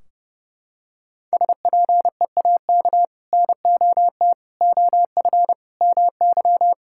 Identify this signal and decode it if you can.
Morse code, heard as a keyed tone, about 30 wpm, 700 Hz. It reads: SPEAK NOT OF MY
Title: Sentences from Top 1600 Words 30wpm